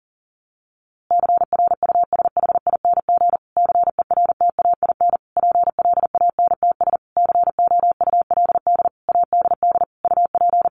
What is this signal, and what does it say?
Morse code, heard as a keyed tone, about 40 wpm, 700 Hz. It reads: CRUSHING CERTAIN PLANTS COULD ADD UP
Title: A Sound Of Thunder 40wpm